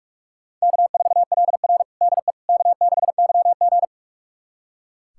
Morse code, heard as a keyed tone, about 42 wpm, 700 Hz. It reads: K4LR DE K6YG
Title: USA Callsign DE Callsign - 4 Characters - 45wpm